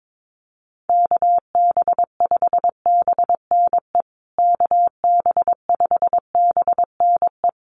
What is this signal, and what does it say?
Morse code, heard as a keyed tone, about 22 wpm, 700 Hz. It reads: KB5BNE KB5BNE
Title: USA Call Signs Repeated 3x 22wpm